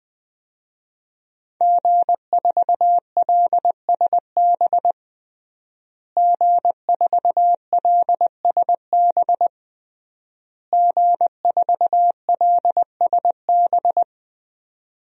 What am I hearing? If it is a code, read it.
Morse code, heard as a keyed tone, about 20 wpm, 700 Hz. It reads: G4LSB G4LSB G4LSB
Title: DXCC-100 Call Signs - 5 Characters - Repeated 3x - 20wpm